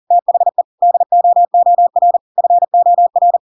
Morse code, heard as a keyed tone, about 40 wpm, 700 Hz. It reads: THE DOOR FOR
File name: Pinocchio / Pinocchio ch7 40 40 ik1hge